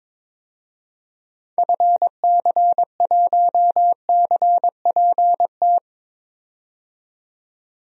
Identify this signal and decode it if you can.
Morse code, heard as a keyed tone, about 21 wpm, 700 Hz. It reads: FC1CPT
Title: DXCC-100 Call Signs - 6 Characters - 22wpm